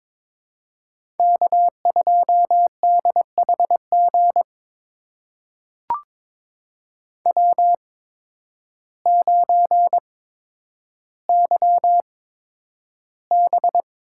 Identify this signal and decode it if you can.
Morse code, heard as a keyed tone, about 22 wpm, spaced as for 10 wpm, 700 Hz. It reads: K2DHG W9YB
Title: USA Call Signs - 5 Characters - 8x Character Spacing - 22wpm